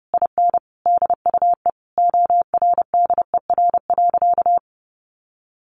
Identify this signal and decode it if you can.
Morse code, heard as a keyed tone, about 30 wpm, 700 Hz. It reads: IN DUE ORDER.
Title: Sentences from Top 900 Words 30wpm